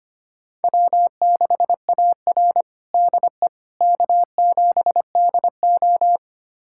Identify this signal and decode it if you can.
Morse code, heard as a keyed tone, about 25 wpm, 700 Hz. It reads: W6AR DE K7DO
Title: USA Callsign DE Callsign - 4 Characters - 25wpm